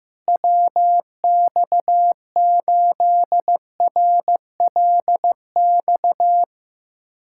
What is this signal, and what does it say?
Morse code, heard as a keyed tone, about 15 wpm, 700 Hz. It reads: WX8RLX
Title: USA Callsign DE Callsign 15wpm